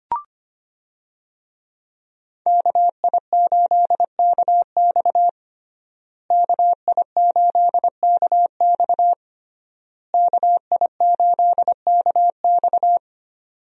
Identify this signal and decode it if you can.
Morse code, heard as a keyed tone, about 25 wpm, 700 Hz. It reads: KI8KX KI8KX KI8KX
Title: USA Call Signs - 5 Characters - Repeated 3x - 25wpm